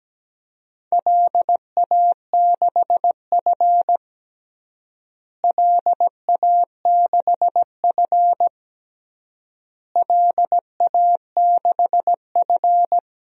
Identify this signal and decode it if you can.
Morse code, heard as a keyed tone, about 17 wpm, 700 Hz. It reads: LA6F LA6F LA6F
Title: DXCC-100 Call Signs - 4 Characters - Repeated 3x - 17wpm